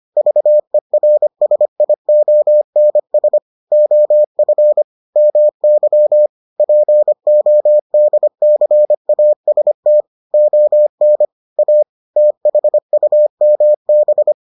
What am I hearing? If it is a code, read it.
Morse code, heard as a keyed tone, about 25 wpm, 600 Hz. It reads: VERSIONS OF MY PODCAST ON A THUMB